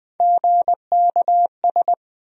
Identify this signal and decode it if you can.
Morse code, heard as a keyed tone, about 20 wpm, 700 Hz. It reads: GKS
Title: USA Call Sign Postfixes - 3 Letters 20wpm